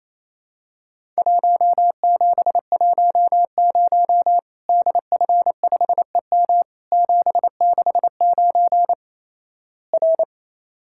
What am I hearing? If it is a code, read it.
Morse code, heard as a keyed tone, about 28 wpm, 700 Hz. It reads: 1710 DF5EM 769 R
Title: Worked All Europe DX Contest - QTC Exchange - 28wpm